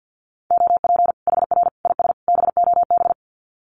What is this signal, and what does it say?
Morse code, heard as a keyed tone, about 49 wpm, 700 Hz. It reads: OP HR IS BOB
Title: Common QSO Phrases 50wpm